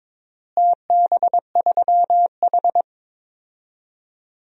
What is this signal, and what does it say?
Morse code, heard as a keyed tone, about 22 wpm, 700 Hz. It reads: TB3H
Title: DXCC-100 Call Signs - 4 Characters - Repeated 3x - 22wpm